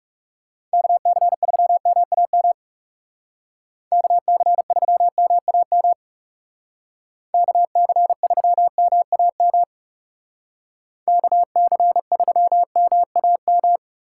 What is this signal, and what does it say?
Morse code, heard as a keyed tone, about 39 wpm, 700 Hz. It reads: KC3MAM KC3MAM KC3MAM KC3MAM